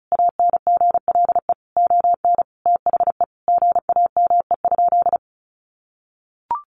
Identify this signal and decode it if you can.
Morse code, heard as a keyed tone, about 34 wpm, 700 Hz. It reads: ANGLE ON THE GAME?
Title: Sentences from Top 1500 Words 35wpm